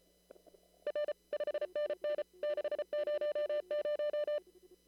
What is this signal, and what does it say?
Morse code, heard as a keyed tone, about 34 wpm, 590 Hz. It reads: R 5NN 600
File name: CW / kt5j wpx cw-op-k5pi